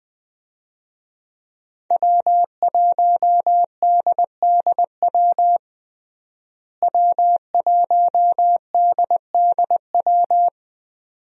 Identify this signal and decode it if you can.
Morse code, heard as a keyed tone, about 20 wpm, 700 Hz. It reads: W1DDW W1DDW
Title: USA Call Signs - 5 Characters - Repeated 2x - 20wpm